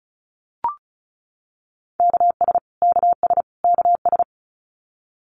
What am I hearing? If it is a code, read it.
Morse code, heard as a keyed tone, about 34 wpm, 700 Hz. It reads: KS KS KS
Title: US States Repeated 3x 35wpm